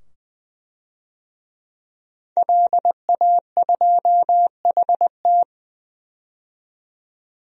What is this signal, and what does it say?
Morse code, heard as a keyed tone, about 20 wpm, 700 Hz. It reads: LA2HT
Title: DXCC-100 Call Signs - 5 Characters - 20wpm